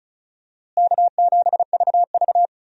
Morse code, heard as a keyed tone, about 34 wpm, 700 Hz. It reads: K7VV
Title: USA Call Signs - 4 Characters - Repeated 3x - 35wpm